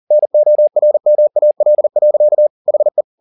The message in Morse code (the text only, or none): NORMAL. HE